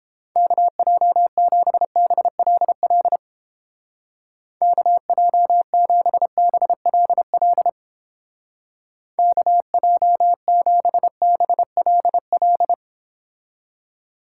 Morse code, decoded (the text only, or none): KJ7BLL KJ7BLL KJ7BLL